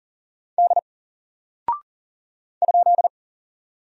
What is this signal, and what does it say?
Morse code, heard as a keyed tone, about 40 wpm, 700 Hz. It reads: D ?
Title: Single Character (Letter:Number:Punctuation:Prosign) Alt 40wpm